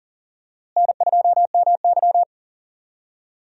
Morse code, heard as a keyed tone, about 40 wpm, 700 Hz. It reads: N2MY
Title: USA Call Signs - 4 Characters - Repeated 2x - 40wpm